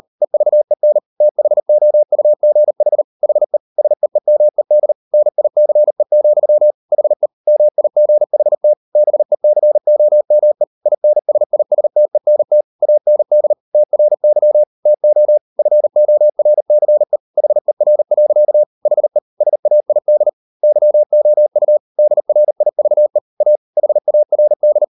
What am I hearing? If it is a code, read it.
Morse code, heard as a keyed tone, about 38 wpm, 600 Hz. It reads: EVEN THOUGH HE SEEMED NICE, HE MIGHT BECOME INSISTENT AND TRY TO FORCE HER. HE SAID YOU DRIVE A HARD